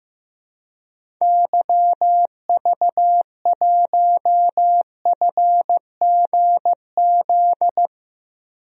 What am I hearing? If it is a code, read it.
Morse code, heard as a keyed tone, about 15 wpm, 700 Hz. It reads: YV1FGZ
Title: DXCC-100 Call Signs - 6 Characters - Repeated 3x - 15wpm